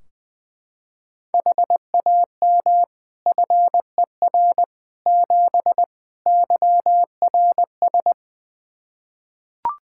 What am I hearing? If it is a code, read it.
Morse code, heard as a keyed tone, about 20 wpm, 700 Hz. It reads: HAM FER 7 YRS
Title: Practice QSO - Exchange 3 - More Info - 20wpm